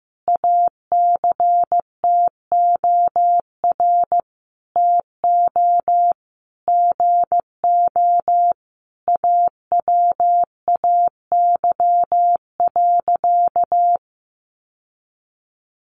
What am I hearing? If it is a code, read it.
Morse code, heard as a keyed tone, about 14 wpm, 700 Hz. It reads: ACTOR TO GO AWAY.
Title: Sentences from Top 1500 Words 15wpm